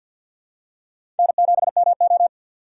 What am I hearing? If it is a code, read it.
Morse code, heard as a keyed tone, about 52 wpm, 700 Hz. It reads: N7MO
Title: USA Call Signs - 4 Characters - 6x Character Spacing - 50wpm